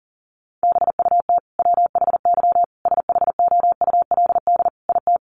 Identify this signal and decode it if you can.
Morse code, heard as a keyed tone, about 40 wpm, 700 Hz. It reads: BUT WHY SHOULD IT
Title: Sentences from Top 1200 Words 40wpm